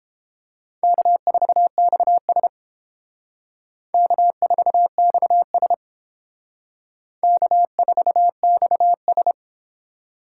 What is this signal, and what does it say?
Morse code, heard as a keyed tone, about 30 wpm, 700 Hz. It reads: K4XS K4XS K4XS